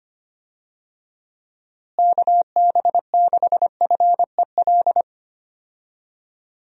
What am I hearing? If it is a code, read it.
Morse code, heard as a keyed tone, about 25 wpm, 700 Hz. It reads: KB6FEL
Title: USA Call Signs - 6 Characters - Repeated 2x - 25wpm